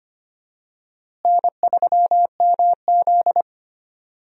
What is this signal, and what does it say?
Morse code, heard as a keyed tone, about 25 wpm, 700 Hz. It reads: N3MZ
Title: USA Call Signs - 4 Characters - Repeated 2x - 25wpm